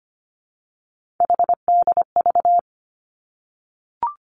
Morse code, heard as a keyed tone, about 25 wpm, 700 Hz. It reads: HDV
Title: Three Challenging Letter-Number Combinations 25wpm